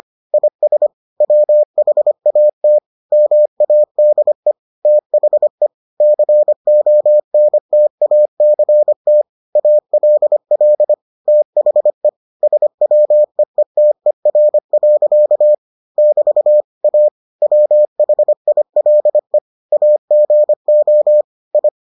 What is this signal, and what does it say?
Morse code, heard as a keyed tone, about 25 wpm, 600 Hz. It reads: IS WHAT MADE THE CONTACT ALL THE SWEETER. <BT> A WHILE AGO I